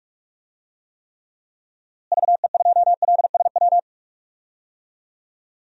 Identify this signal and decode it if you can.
Morse code, heard as a keyed tone, about 41 wpm, 700 Hz. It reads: VE2LSW
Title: DXCC-100 Call Signs - 6 Characters - Repeated 2x - 45wpm